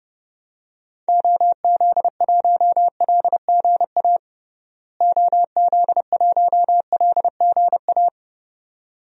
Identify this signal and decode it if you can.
Morse code, heard as a keyed tone, about 30 wpm, 700 Hz. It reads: OZ1LGA OZ1LGA
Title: DXCC-100 Call Signs - 6 Characters - Repeated 2x - 30wpm